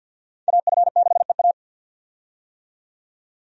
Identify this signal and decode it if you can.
Morse code, heard as a keyed tone, about 50 wpm, 700 Hz. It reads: AF6EA
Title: USA Call Signs - 5 Characters - 50wpm